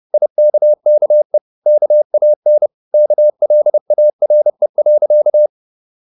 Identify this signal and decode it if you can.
Morse code, heard as a keyed tone, about 30 wpm, 600 Hz. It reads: IKKE KAN KLARE.
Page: Ferie 30 wpm